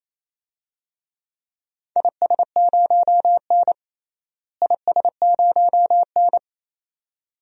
Morse code, heard as a keyed tone, about 28 wpm, 700 Hz. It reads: IS0N IS0N
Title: DXCC-100 Call Signs - 4 Characters - Repeated 2x - 28wpm